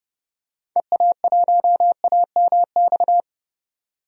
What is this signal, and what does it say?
Morse code, heard as a keyed tone, about 31 wpm, 700 Hz. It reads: EA1AMX
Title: DXCC-100 Call Signs - 6 Characters - 30wpm